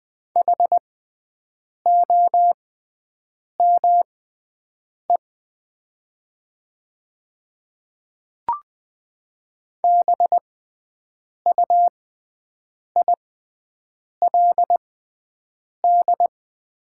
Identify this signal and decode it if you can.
Morse code, heard as a keyed tone, about 20 wpm, spaced as for 7 wpm, 700 Hz. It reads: HOME BUILD
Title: 6x Character Spacing - Top 200 Words - 20wpm